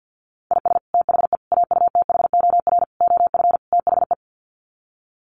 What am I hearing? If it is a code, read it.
Morse code, heard as a keyed tone, about 50 wpm, 700 Hz. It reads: IS THE AUTHOR OF THE
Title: Sets of 5 Words Encore 50wpm